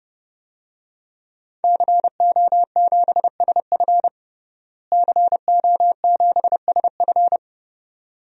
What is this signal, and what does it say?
Morse code, heard as a keyed tone, about 30 wpm, 700 Hz. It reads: CO7SF CO7SF
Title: DXCC-100 Call Signs - 5 Characters - Repeated 2x - 30wpm